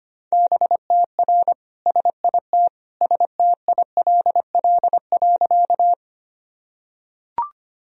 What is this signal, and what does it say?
Morse code, heard as a keyed tone, about 25 wpm, 700 Hz. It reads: BTR SIT STILL.
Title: Sentences from Top 500 Words and Abbreviations 25wpm